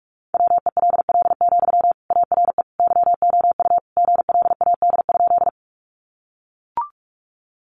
Morse code, heard as a keyed tone, about 44 wpm, 700 Hz. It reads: WELL, ARE YOU GLAD?
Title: Sentences from Top 800 Words 45wpm